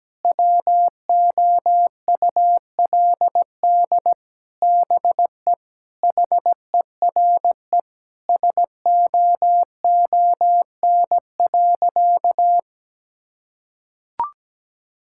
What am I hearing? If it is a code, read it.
Morse code, heard as a keyed tone, about 17 wpm, 700 Hz. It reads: WOULD BE HERE SOON.
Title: Sentences from Top 1600 Words 17wpm